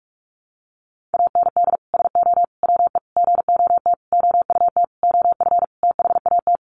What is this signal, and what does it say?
Morse code, heard as a keyed tone, about 44 wpm, 700 Hz. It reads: AND SO WE GOT OUT OF THAT
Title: Sentences from Top 1500 Words 45wpm